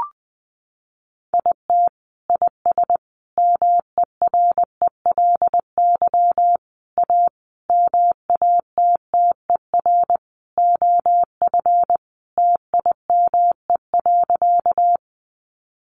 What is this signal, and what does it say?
Morse code, heard as a keyed tone, about 19 wpm, 700 Hz. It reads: IT IS MERELY A MATTER OF TIME.
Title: Sentences from Top 1400 Words 20wpm